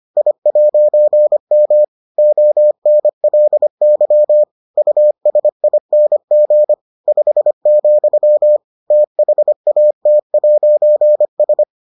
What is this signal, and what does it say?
Morse code, heard as a keyed tone, about 25 wpm, 600 Hz. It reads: I'M ONLY USING 5, THAT'S